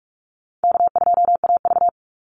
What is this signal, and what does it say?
Morse code, heard as a keyed tone, about 44 wpm, 700 Hz. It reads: K2AV
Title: Easy Call Signs Repeated 2x - 45wpm